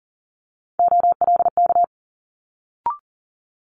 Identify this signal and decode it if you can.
Morse code, heard as a keyed tone, about 41 wpm, 700 Hz. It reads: OLK
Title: Multi-letter Phonemes Quick Fire 40wpm